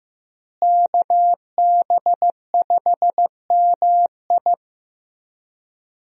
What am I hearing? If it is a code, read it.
Morse code, heard as a keyed tone, about 15 wpm, 700 Hz. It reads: KB5MI
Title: USA Call Signs - 5 Characters - Repeated 3x - 15wpm